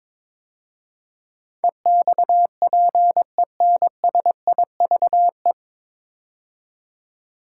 Morse code, heard as a keyed tone, about 22 wpm, 700 Hz. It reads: EXPENSIVE
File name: 9 Letter Words - First 3 Letters Given - 22wpm